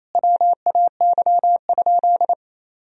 Morse code, heard as a keyed tone, about 27 wpm, 700 Hz. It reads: WAY?
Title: Sentences from Top 1000 Words - Review Part 2 - 28wpm